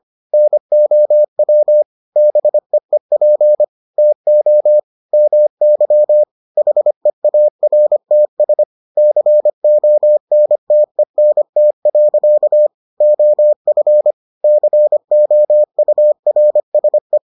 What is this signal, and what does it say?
Morse code, heard as a keyed tone, about 25 wpm, 600 Hz. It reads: NOW BEEP TO MY HEARTS CONTENT. OF COURSE